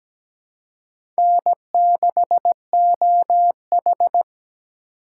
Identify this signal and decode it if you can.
Morse code, heard as a keyed tone, about 17 wpm, 700 Hz. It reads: N6OH
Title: USA Call Signs - 4 Characters - 3x Character Spacing - 17wpm